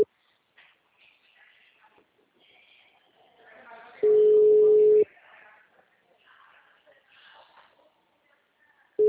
Ringback tone (Europe, 425 Hz).